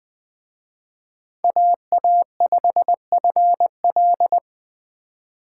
Morse code, heard as a keyed tone, about 21 wpm, 700 Hz. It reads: AA5FL